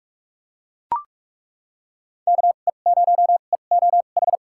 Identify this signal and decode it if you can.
Morse code, heard as a keyed tone, about 41 wpm, spaced as for 31 wpm, 700 Hz. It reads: KE0EOH
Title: USA Call Signs - 6 Characters - 2x Character Spacing - 45wpm